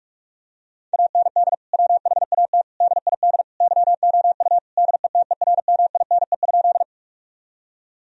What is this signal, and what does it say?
Morse code, heard as a keyed tone, about 41 wpm, 700 Hz. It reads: AND WHAT DID YOU DETERMINE?
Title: Sentences from Top 1000 Words - Review Part 4 - 45wpm